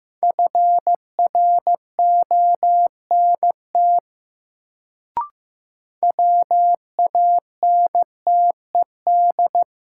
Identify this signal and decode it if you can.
Morse code, heard as a keyed tone, about 15 wpm, 700 Hz. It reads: FRONT WANTED
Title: Sets of 2 Words - New Word Drill - 15wpm